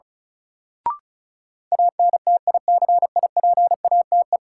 Morse code, heard as a keyed tone, about 34 wpm, 700 Hz. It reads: ANTICIPATE
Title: Sentences from Top 1500 Words - New Word Drill - 35wpm